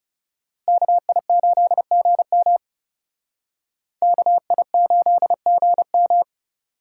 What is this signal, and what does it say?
Morse code, heard as a keyed tone, about 38 wpm, 700 Hz. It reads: KI8GM KI8GM
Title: Speed Racing - USA Call Signs 30wpm